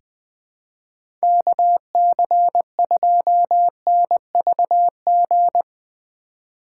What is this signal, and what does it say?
Morse code, heard as a keyed tone, about 20 wpm, 700 Hz. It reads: KC2NVG
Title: USA Call Signs - 4x Character Spacing - 20wpm